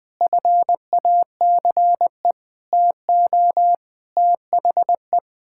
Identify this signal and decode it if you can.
Morse code, heard as a keyed tone, about 20 wpm, 700 Hz. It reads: FACE TO THE
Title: Sets of 8 Words - 20wpm